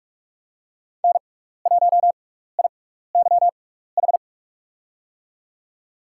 Morse code, heard as a keyed tone, about 46 wpm, spaced as for 16 wpm, 700 Hz. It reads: N1IYH
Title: USA Call Signs - 5 Characters - 6x Character Spacing - 45wpm